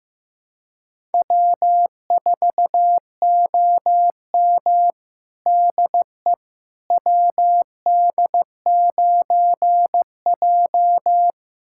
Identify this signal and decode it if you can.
Morse code, heard as a keyed tone, about 15 wpm, 700 Hz. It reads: W4OM DE WD9J
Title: USA Callsign DE Callsign - 4 Characters - 15wpm